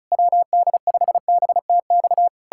Morse code, heard as a keyed tone, about 40 wpm, 700 Hz. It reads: WD5BTX